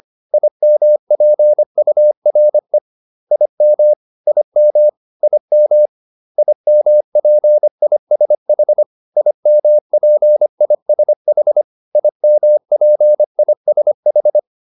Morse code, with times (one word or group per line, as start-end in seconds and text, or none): 0.3-2.8 IMPURE
3.3-3.9 IM
4.3-4.9 IM
5.2-5.9 IM
6.4-8.8 IMPISH
9.2-11.6 IMPISH
12.0-14.4 IMPISH